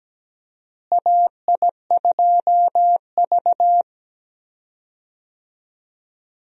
Morse code, heard as a keyed tone, about 17 wpm, 700 Hz. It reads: AI2V
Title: USA Call Signs - 4 Characters - Repeated 3x - 17wpm